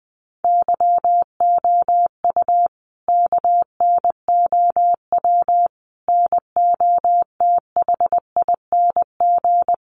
Morse code, heard as a keyed tone, about 20 wpm, 700 Hz. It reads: YOU KNOW NOTHING
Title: Sentences from Top 500 Words 20wpm